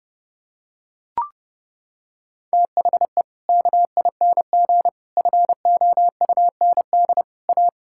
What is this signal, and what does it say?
Morse code, heard as a keyed tone, about 31 wpm, 700 Hz. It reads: THE KING FOUND A
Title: Sentences from Top 1000 Words - Review Part 5 - 30wpm